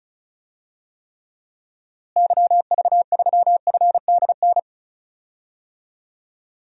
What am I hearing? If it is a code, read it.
Morse code, heard as a keyed tone, about 36 wpm, 700 Hz. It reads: YV3FDN
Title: DXCC-100 Call Signs - 6 Characters - Repeated 3x - 35wpm